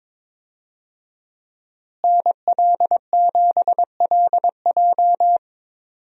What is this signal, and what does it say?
Morse code, heard as a keyed tone, about 22 wpm, 700 Hz. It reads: NL7LJ
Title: USA Call Signs - 5 Characters - Repeated 2x - 22wpm